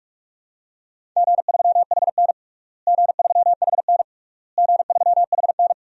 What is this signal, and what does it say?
Morse code, heard as a keyed tone, about 41 wpm, 700 Hz. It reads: G3HN G3HN G3HN
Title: DXCC-100 Call Signs - 4 Characters - Repeated 3x - 45wpm